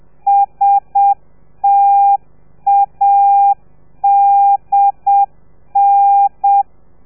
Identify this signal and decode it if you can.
Morse code, heard as a keyed tone, about 7 wpm, 800 Hz. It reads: STADN